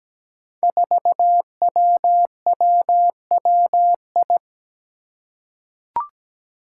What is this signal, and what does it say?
Morse code, heard as a keyed tone, about 17 wpm, 700 Hz. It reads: 4WWWI
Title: Five Challenging Letter-Number Combinations 17wpm